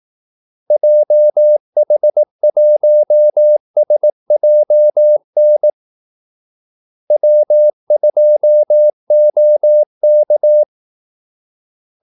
Morse code, heard as a keyed tone, about 18 wpm, 600 Hz. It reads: JH1SJN W2OK
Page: Kaldesignaler 18 wpm